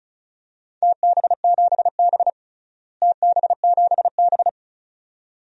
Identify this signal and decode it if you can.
Morse code, heard as a keyed tone, about 34 wpm, 700 Hz. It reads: TB7B TB7B
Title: DXCC-100 Call Signs - 4 Characters - Repeated 2x - 35wpm